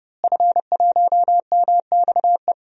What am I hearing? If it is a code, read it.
Morse code, heard as a keyed tone, about 30 wpm, 700 Hz. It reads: F1MXE
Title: DXCC-100 Call Signs - 5 Characters - Repeated 3x - 30wpm